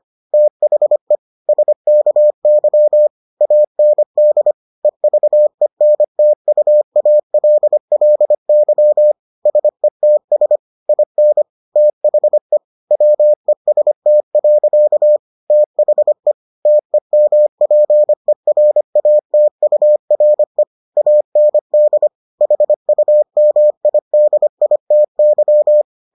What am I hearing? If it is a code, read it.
Morse code, heard as a keyed tone, about 25 wpm, 600 Hz. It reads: THE SKY AND EVENTUALLY SETS IN THE WEST. THE TEMPERATURE AND HUMIDITY